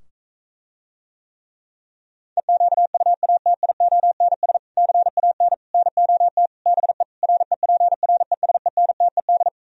Morse code, heard as a keyed tone, about 45 wpm, 700 Hz. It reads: EQUATIONS CAN NOT BE REPRESENTED